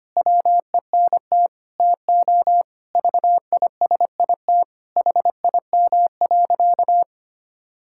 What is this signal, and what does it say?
Morse code, heard as a keyed tone, about 25 wpm, 700 Hz. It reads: WENT TO VISIT HIM.
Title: Sentences from Top 1000 Words - Review Part 6 - 25wpm